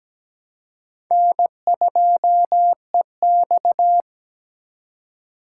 Morse code, heard as a keyed tone, about 17 wpm, 700 Hz. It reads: N2EX
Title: USA Call Signs - 4 Characters - 3x Character Spacing - 17wpm